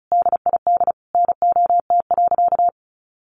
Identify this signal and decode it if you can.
Morse code, heard as a keyed tone, about 34 wpm, 700 Hz. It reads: DID NOT.
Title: Sentences from Top 600 Words 35wpm